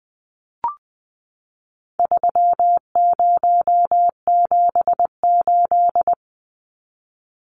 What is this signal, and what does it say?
Morse code, heard as a keyed tone, about 20 wpm, 700 Hz. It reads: 3078
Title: Numbers - 4-Digit 20wpm